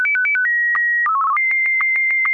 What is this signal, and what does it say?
SSTV (slow-scan television), Martin 1.